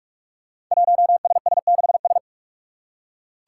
Morse code, heard as a keyed tone, about 44 wpm, 700 Hz. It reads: 1SS6S
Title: Five Challenging Letter-Number Combinations 45wpm